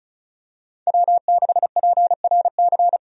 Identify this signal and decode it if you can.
Morse code, heard as a keyed tone, about 36 wpm, 700 Hz. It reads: W6PRC
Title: USA Call Signs - 5 Characters - 35wpm